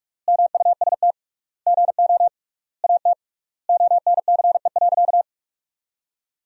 Morse code, heard as a keyed tone, about 47 wpm, 700 Hz. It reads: MUST GO AT ONCE.
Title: 3x Word-Spacing - Sentences from Top 300 Words - 45wpm